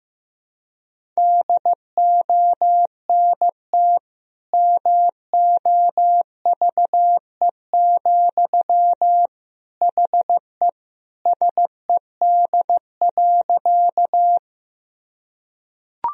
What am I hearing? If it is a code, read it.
Morse code, heard as a keyed tone, about 16 wpm, 700 Hz. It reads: DONT MOVE, HE SED.
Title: Sentences from Top 300 Words and Abbreviations 15wpm